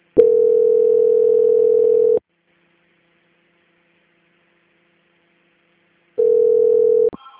Ringback tone (North America), 440 + 480 Hz.